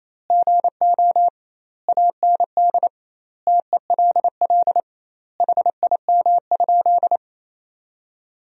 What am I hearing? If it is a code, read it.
Morse code, heard as a keyed tone, about 27 wpm, 700 Hz. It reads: GO AND TELL HIM?
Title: 2x Word-Spacing - Sentences from Top 200 Words - 28wpm